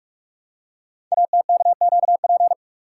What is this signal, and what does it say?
Morse code, heard as a keyed tone, about 44 wpm, 700 Hz. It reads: ATKQP
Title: Code Groups - 5 Letters - 45wpm